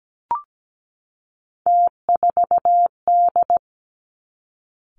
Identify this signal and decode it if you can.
Morse code, heard as a keyed tone, about 17 wpm, 700 Hz. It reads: T4D
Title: Three Challenging Letter-Number Combinations 17wpm